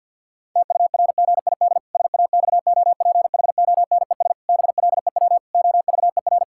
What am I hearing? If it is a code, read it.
Morse code, heard as a keyed tone, about 50 wpm, 700 Hz. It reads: TURGID SAXOPHONES BLEW OVER
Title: Pangram Challenge v2 - 50wpm